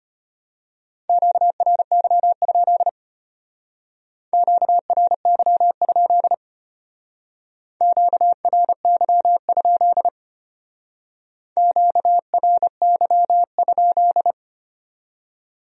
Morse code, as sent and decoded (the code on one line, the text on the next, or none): --.- .-. -.-- ..--.. / --.- .-. -.-- ..--.. / --.- .-. -.-- ..--.. / --.- .-. -.-- ..--..
QRY? QRY? QRY? QRY?